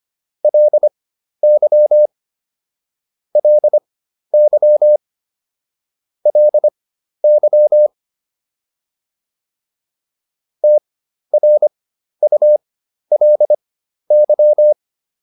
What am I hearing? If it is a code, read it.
Morse code, heard as a keyed tone, about 25 wpm, spaced as for 12 wpm, 600 Hz. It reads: LY LY LY TRULY